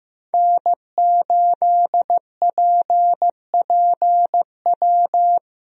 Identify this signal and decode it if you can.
Morse code, heard as a keyed tone, about 16 wpm, 700 Hz. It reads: N8PPW
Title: USA Call Signs - 5 Characters - Repeated 3x - 15wpm